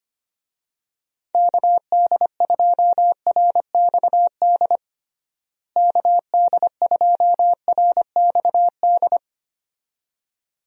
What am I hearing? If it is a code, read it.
Morse code, heard as a keyed tone, about 25 wpm, 700 Hz. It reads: KD2RXD KD2RXD
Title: USA Call Signs - 6 Characters - Repeated 2x - 25wpm